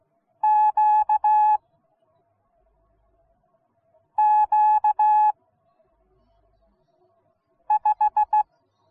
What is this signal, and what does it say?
Morse code, heard as a keyed tone, about 16 wpm, 850 Hz. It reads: Q Q 5